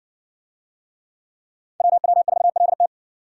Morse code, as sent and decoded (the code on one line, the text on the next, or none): .-- .-- ....- .-.. -
WW4LT